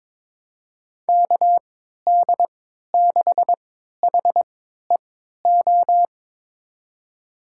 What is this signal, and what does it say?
Morse code, heard as a keyed tone, about 22 wpm, spaced as for 13 wpm, 700 Hz. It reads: KD6HEO
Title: USA Call Signs - 3x Character Spacing - 22wpm